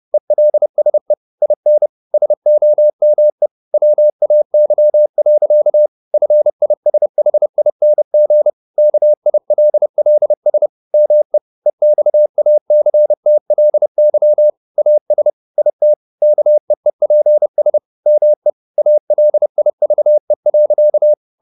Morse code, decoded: ELSE IN SOME WAY. FISHING KILLS ME EXACTLY AS IT KEEPS ME ALIVE.

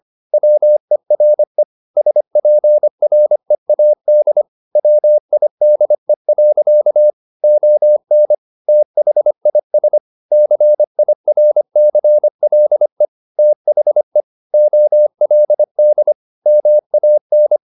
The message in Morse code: WERE SPREAD WIDE. ON THIS CIRCLE THE OLD MAN

